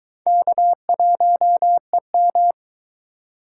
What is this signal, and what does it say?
Morse code, heard as a keyed tone, about 23 wpm, 700 Hz. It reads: K1EM